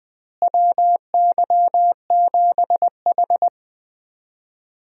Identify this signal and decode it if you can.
Morse code, heard as a keyed tone, about 20 wpm, 700 Hz. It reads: WY7H
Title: USA Call Signs - 4 Characters - Repeated 3x - 20wpm